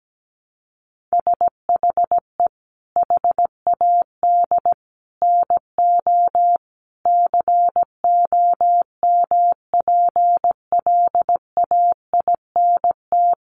Morse code, heard as a keyed tone, about 17 wpm, 700 Hz. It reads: SHE HAD NO COMPLAINT
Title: Sentences from Top 1400 Words 17wpm